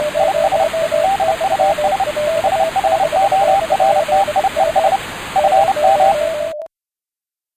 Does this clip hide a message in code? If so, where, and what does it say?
Yes: Morse code from 0.2 to 6.1 s, heard as a keyed tone, about 30 wpm, 780 Hz. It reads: IS DUS SHORTIES FM